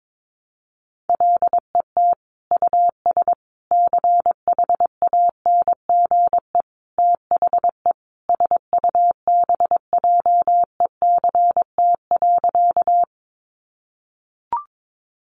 Morse code, heard as a keyed tone, about 22 wpm, 700 Hz. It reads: LET US CHANGE THE SUBJECT.
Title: Sentences from Top 800 Words 22wpm